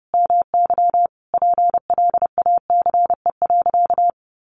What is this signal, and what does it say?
Morse code, heard as a keyed tone, about 30 wpm, 700 Hz. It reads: MY PLACE.
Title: Sentences from Top 1200 Words 30wpm